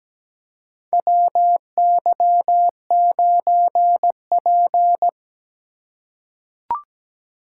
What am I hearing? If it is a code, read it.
Morse code, heard as a keyed tone, about 17 wpm, 700 Hz. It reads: WY9P
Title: USA Call Signs - 4 Characters - 3x Character Spacing - 17wpm